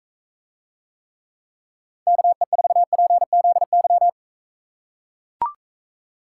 Morse code, heard as a keyed tone, about 42 wpm, 700 Hz. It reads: KE4PZY